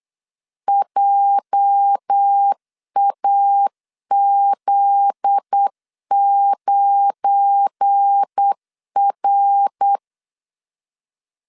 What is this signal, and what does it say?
Morse code, heard as a keyed tone, about 8 wpm, 800 Hz. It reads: JAZ9R
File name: LEZIONI / 8 YJ45 / 8 YJ 8wpm 3313